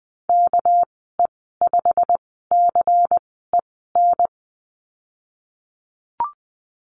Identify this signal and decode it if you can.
Morse code, heard as a keyed tone, about 20 wpm, spaced as for 14 wpm, 700 Hz. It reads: KE5CEN
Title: 238 Lesson 38g - Letter q - Call Signs 20wpm